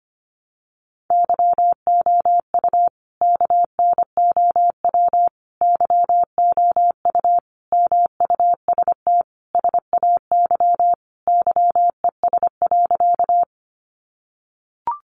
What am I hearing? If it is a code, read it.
Morse code, heard as a keyed tone, about 25 wpm, 700 Hz. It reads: YOU KNOW YOU MUST SAY YES.
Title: Sentences from Top 500 Words 25wpm